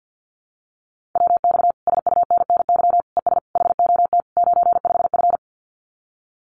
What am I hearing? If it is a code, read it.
Morse code, heard as a keyed tone, about 50 wpm, 700 Hz. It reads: WX SUNNY ES HOT 95F
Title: Common QSO Phrases Encore 50wpm